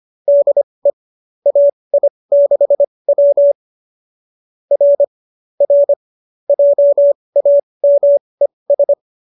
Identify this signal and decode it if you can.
Morse code, heard as a keyed tone, about 25 wpm, spaced as for 20 wpm, 590 Hz. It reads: DE AI6W R R JAMES
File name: qso207b 20